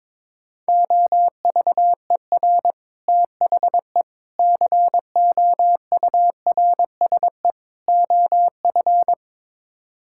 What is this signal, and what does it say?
Morse code, heard as a keyed tone, about 22 wpm, 700 Hz. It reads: OVER THE COURSE OF
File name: practice / 22wpm / Build A Phrase - 5-Word Phrases - 22wpm